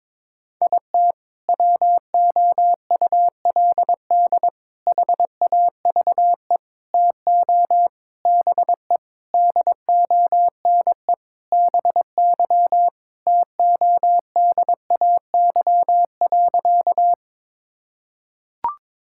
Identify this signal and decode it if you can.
Morse code, heard as a keyed tone, about 22 wpm, 700 Hz. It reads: IT WOULD HAVE TO BE DONE BY TODAY.
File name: practice / 22wpm / Sentences with 4-Word Phrases - 22wpm